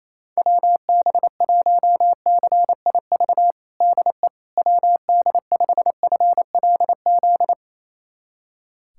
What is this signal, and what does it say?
Morse code, heard as a keyed tone, about 28 wpm, 700 Hz. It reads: WB1CIV DE WD5FLZ
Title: USA Callsign DE Callsign - 6 Characters - 28wpm